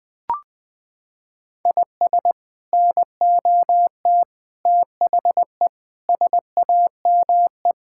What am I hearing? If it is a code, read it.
Morse code, heard as a keyed tone, about 20 wpm, 700 Hz. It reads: IS NOT THE SAME
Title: Sets of 6 Words - 20wpm